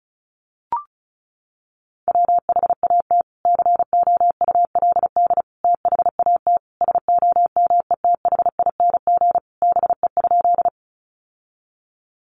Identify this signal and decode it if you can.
Morse code, heard as a keyed tone, about 34 wpm, 700 Hz. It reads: WHAT COULD THAT SOMETHING BE?
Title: Sentences from Top 600 Words 35wpm